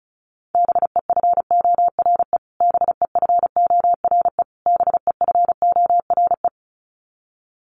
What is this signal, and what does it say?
Morse code, heard as a keyed tone, about 34 wpm, 700 Hz. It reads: BEFORE BEFORE BEFORE
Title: Top 500 Words Repeated 35wpm